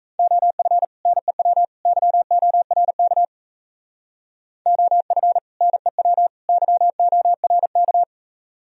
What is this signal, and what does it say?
Morse code, heard as a keyed tone, about 32 wpm, 700 Hz. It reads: OF NEW YORK OF NEW YORK
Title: Speed-Racing - Sets of 3 Words 28wpm